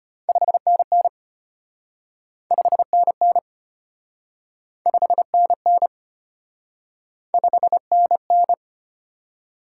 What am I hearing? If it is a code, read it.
Morse code, heard as a keyed tone, about 34 wpm, 700 Hz. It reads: 5NN 5NN 5NN 5NN